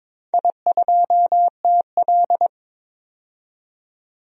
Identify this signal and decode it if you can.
Morse code, heard as a keyed tone, about 22 wpm, 700 Hz. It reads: I2TL
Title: DXCC-100 Call Signs - 4 Characters - 22wpm